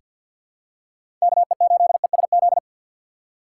Morse code, heard as a keyed tone, about 47 wpm, 700 Hz. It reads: KE8ESZ
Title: USA Call Signs - 6x Character Spacing - 50wpm